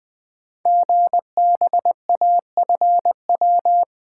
Morse code, heard as a keyed tone, about 20 wpm, 700 Hz. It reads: GBAFW
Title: Code Groups - 5 Letters - 20wpm